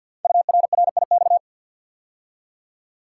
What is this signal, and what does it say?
Morse code, heard as a keyed tone, about 48 wpm, 700 Hz. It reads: URRIX
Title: Code Groups - 5 Letters - 50wpm